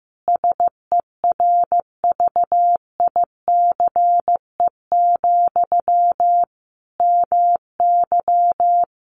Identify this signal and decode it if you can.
Morse code, heard as a keyed tone, about 15 wpm, 700 Hz. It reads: SERVICE, MY
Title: Sentences from Top 1100 Words 15wpm